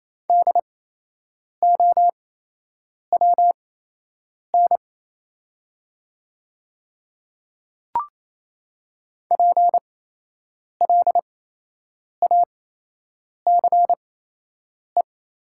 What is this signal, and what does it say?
Morse code, heard as a keyed tone, about 28 wpm, spaced as for 8 wpm, 700 Hz. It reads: DOWN PLACE
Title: 8x Character Spacing - Top 200 Words - 28wpm